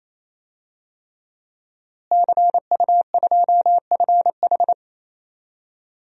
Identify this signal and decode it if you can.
Morse code, heard as a keyed tone, about 27 wpm, 700 Hz. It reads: CU2FH
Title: DXCC-100 Call Signs - 5 Characters - Repeated 3x - 28wpm